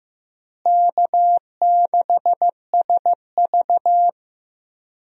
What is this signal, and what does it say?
Morse code, heard as a keyed tone, about 15 wpm, 700 Hz. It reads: K6SV
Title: USA Call Signs - 4 Characters - Repeated 2x - 15wpm